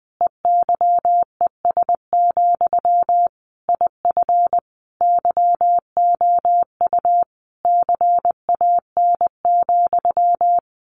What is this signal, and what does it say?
Morse code, heard as a keyed tone, about 20 wpm, 700 Hz. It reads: EYES, IF YOU CAN,
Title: A Sound Of Thunder 20wpm